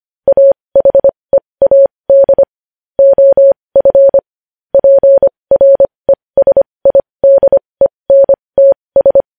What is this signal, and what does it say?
Morse code, heard as a keyed tone, about 25 wpm, spaced as for 20 wpm, 550 Hz. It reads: AHEAD OF PRESIDENTS
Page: news-headlines-in-morse-code-at-20-wpm